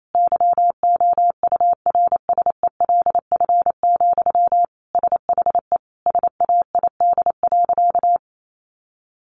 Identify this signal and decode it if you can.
Morse code, heard as a keyed tone, about 28 wpm, 700 Hz. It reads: YOURSELF, SHE SAID.
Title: Sentences from Top 600 Words 28wpm